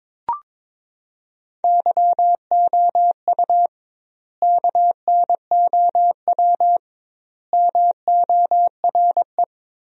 Morse code, heard as a keyed tone, about 22 wpm, 700 Hz. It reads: YOU KNOW MORE
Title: 2x Word-Spacing - Sentences from Top 100 Words - 22wpm